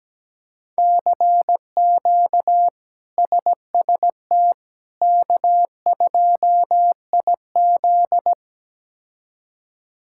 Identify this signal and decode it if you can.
Morse code, heard as a keyed tone, about 17 wpm, 700 Hz. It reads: CQ SST K2IZ
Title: K1USN SST QSO Practice - 17wpm